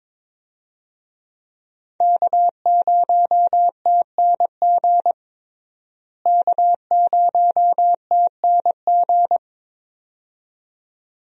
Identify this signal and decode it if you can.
Morse code, heard as a keyed tone, about 22 wpm, 700 Hz. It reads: K0TNG K0TNG
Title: USA Call Signs - 5 Characters - Repeated 2x - 22wpm